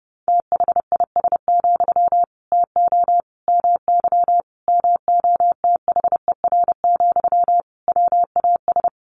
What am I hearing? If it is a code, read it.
Morse code, heard as a keyed tone, about 30 wpm, 700 Hz. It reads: THIS, TO MY MOTHER, WAS
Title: Sentences from Top 1500 Words 30wpm